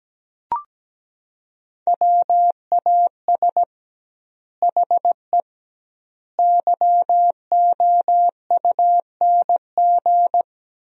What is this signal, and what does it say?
Morse code, heard as a keyed tone, about 17 wpm, 700 Hz. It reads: WAS HE YOUNG
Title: 2x Word-Spacing - Sentences from Top 400 Words - 17wpm